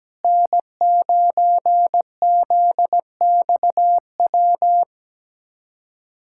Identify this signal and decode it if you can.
Morse code, heard as a keyed tone, about 17 wpm, 700 Hz. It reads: N9ZXW
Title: SOTA QSO Practice 17wpm